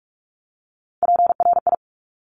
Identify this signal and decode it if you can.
Morse code, heard as a keyed tone, about 46 wpm, 690 Hz. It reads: PRI
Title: Top Three Letter Combinations in Words 45wpm